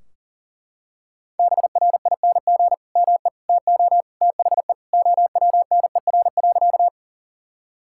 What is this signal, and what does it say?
Morse code, heard as a keyed tone, about 40 wpm, 700 Hz. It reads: BRING ME TO THE OWNER.
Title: Sentences from Top 1600 Words 40wpm